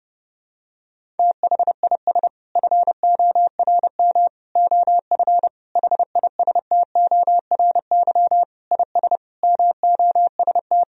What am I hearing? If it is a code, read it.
Morse code, heard as a keyed tone, about 32 wpm, 700 Hz. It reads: THIS FORM OF HISTORY IS MOST
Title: Sentences from Top 1000 Words - Review Part 4 - 30wpm